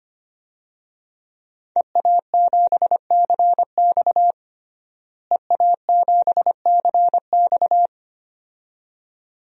Morse code, heard as a keyed tone, about 25 wpm, 700 Hz. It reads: EA7CX EA7CX
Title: DXCC-100 Call Signs - 5 Characters - Repeated 2x - 25wpm